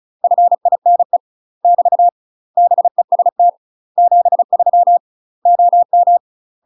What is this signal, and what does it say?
Morse code, heard as a keyed tone, about 35 wpm, 700 Hz. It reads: FINE <BT> BEST 73 OM
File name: UK MASTER QSOs TRAINING / 35WPM / UK MASTER QSOs 0033